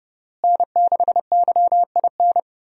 Morse code, heard as a keyed tone, about 29 wpm, 700 Hz. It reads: N6YIN
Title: USA Call Signs - 5 Characters - Repeated 2x - 30wpm